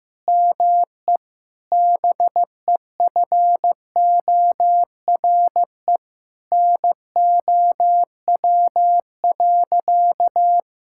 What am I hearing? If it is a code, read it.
Morse code, heard as a keyed tone, about 15 wpm, 700 Hz. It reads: ME BEFORE NOW.
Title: Sentences from Top 1000 Words - Review Part 6 - 15wpm